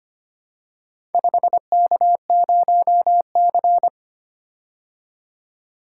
Morse code, heard as a keyed tone, about 25 wpm, 700 Hz. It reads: 5K0C
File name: practice / 25wpm / Four Letter-Number Combinations - 25wpm